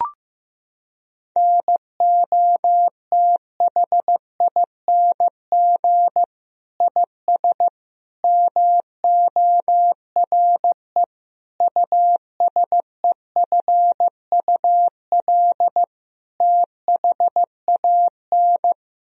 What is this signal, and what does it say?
Morse code, heard as a keyed tone, about 15 wpm, 700 Hz. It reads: NOTHING IS MORE USEFUL THAN
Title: Sentences from Top 1000 Words - Review Part 4 - 15wpm